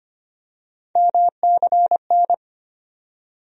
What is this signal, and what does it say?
Morse code, heard as a keyed tone, about 25 wpm, 700 Hz. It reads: MCN
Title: USA Call Sign Postfixes - 3 Letters 25wpm